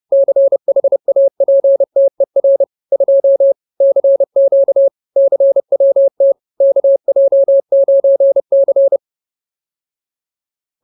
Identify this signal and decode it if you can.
Morse code, heard as a keyed tone, about 30 wpm, 550 Hz. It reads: CHAPTER 2 CQ CWT KJ9C
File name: CWT-Chapter-02